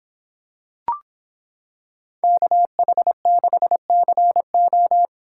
Morse code, heard as a keyed tone, about 26 wpm, 700 Hz. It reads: KH6CO